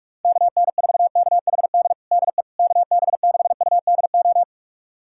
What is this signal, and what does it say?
Morse code, heard as a keyed tone, about 46 wpm, 700 Hz. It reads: KN4KHD DE KB6UDO
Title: USA Callsign DE Callsign 45wpm